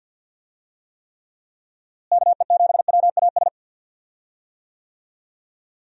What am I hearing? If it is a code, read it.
Morse code, heard as a keyed tone, about 52 wpm, 700 Hz. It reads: KE7WAS
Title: USA Call Signs - 6 Characters - Repeated 3x - 50wpm